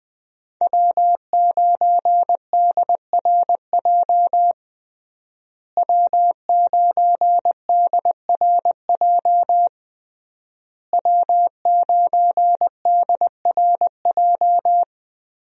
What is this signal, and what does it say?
Morse code, heard as a keyed tone, about 21 wpm, 700 Hz. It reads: W9DRJ W9DRJ W9DRJ
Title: USA Call Signs - 5 Characters - Repeated 3x - 20wpm